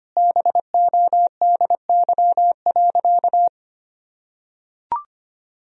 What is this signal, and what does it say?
Morse code, heard as a keyed tone, about 25 wpm, 700 Hz. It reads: BODY.
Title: Sentences from Top 1000 Words - Review Part 2 - 25wpm